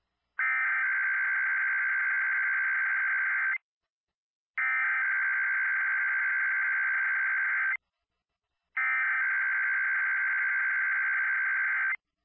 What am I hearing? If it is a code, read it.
Emergency Alert System (SAME) tones (header bursts), heard as harsh buzzing bursts. It is heard from 0.4 to 11.9 s.